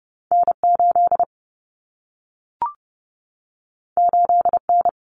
Morse code, heard as a keyed tone, about 30 wpm, 700 Hz. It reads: N8 8N
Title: Common Cut Numbers - 2-Digit 30wpm